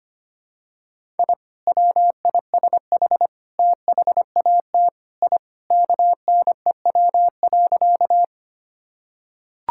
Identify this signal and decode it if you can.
Morse code, heard as a keyed tone, about 25 wpm, 690 Hz. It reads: I WISH THAT I KNEW.
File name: practice / 25wpm / Sentences with 2-Word Phrases - 25wpm